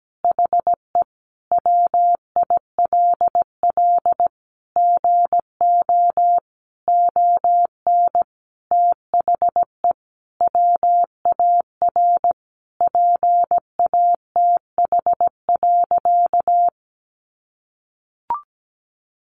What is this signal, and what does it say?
Morse code, heard as a keyed tone, about 17 wpm, 700 Hz. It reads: HE WILL GO ON THE WAR PATH.
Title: Sentences from Top 1400 Words 17wpm